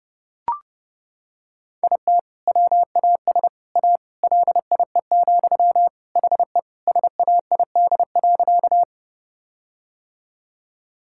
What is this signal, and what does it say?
Morse code, heard as a keyed tone, about 30 wpm, 700 Hz. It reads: IT WAS A LIE, HE SAID.
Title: Sentences from Top 1600 Words 30wpm